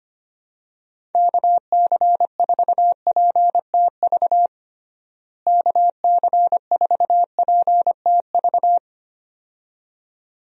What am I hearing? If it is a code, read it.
Morse code, heard as a keyed tone, about 25 wpm, 700 Hz. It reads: KC4PTV KC4PTV
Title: USA Call Signs - 6 Characters - Repeated 2x - 25wpm